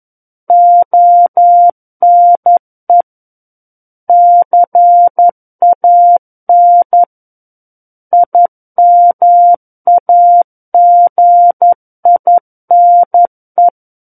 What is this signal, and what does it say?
Morse code, heard as a keyed tone, about 11 wpm, 700 Hz. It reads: ONE CAN IMAGINE
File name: 11 WPM